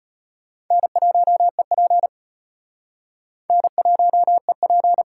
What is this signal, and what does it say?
Morse code, heard as a keyed tone, about 31 wpm, 700 Hz. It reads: N1EP N1EP